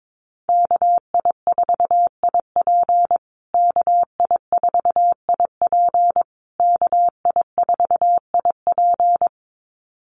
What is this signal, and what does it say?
Morse code, heard as a keyed tone, about 22 wpm, 700 Hz. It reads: KI4IP KI4IP KI4IP
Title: USA Call Signs Repeated 3x 22wpm